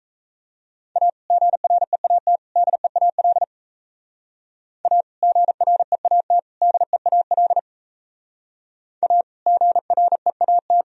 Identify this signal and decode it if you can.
Morse code, heard as a keyed tone, about 36 wpm, 700 Hz. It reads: A GREAT DEAL A GREAT DEAL A GREAT